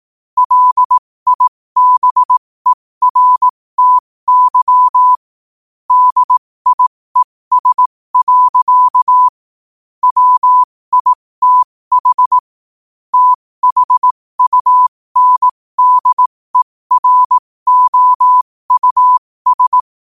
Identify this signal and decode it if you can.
Morse code, heard as a keyed tone, about 18 wpm, spaced as for 15 wpm, 1000 Hz. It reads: LIBERTY DIES. WITH THUNDEROUS
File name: qotdfiles / 15wpm